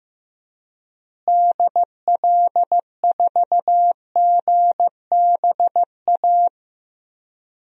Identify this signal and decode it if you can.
Morse code, heard as a keyed tone, about 15 wpm, 700 Hz. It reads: DL4GBA
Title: DXCC-100 Call Signs - 6 Characters - Repeated 3x - 15wpm